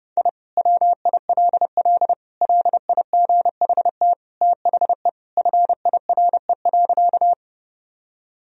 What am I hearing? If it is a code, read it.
Morse code, heard as a keyed tone, about 32 wpm, 700 Hz. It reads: I WILL LIGHT THE FIRE.
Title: Sentences from Top 1000 Words - Review Part 1 - 30wpm